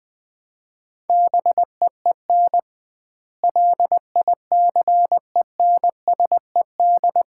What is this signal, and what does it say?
Morse code, heard as a keyed tone, about 20 wpm, 700 Hz. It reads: BEEN LICENSED
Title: Common QSO Phrases - 2x Word-Spacing - 20wpm